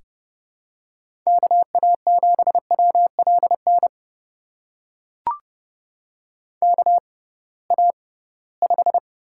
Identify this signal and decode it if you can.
Morse code, heard as a keyed tone, about 31 wpm, spaced as for 20 wpm, 700 Hz. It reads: KA7WLN KA5
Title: USA Call Signs - 6 Characters - 6x Character Spacing - 30wpm